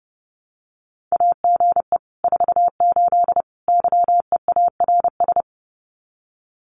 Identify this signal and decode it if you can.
Morse code, heard as a keyed tone, about 30 wpm, 700 Hz. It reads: AGE 48 YEARS
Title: Common QSO Phrases Encore 30wpm